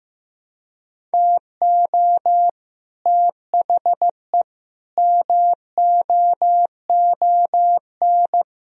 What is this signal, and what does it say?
Morse code, heard as a keyed tone, about 15 wpm, 700 Hz. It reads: TO THE MOON
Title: Sentences from Top 1000 Words - Review Part 3 - 15wpm